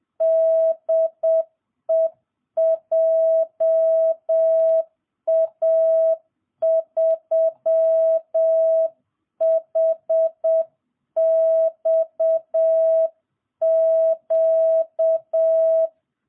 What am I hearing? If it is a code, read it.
Morse code, heard as a keyed tone, about 7 wpm, 650 Hz. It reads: DE JA3HXQ